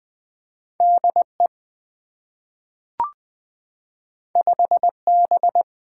Morse code, heard as a keyed tone, about 20 wpm, 700 Hz. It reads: D E 5 B
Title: Two Letter-Number Combinations - 20wpm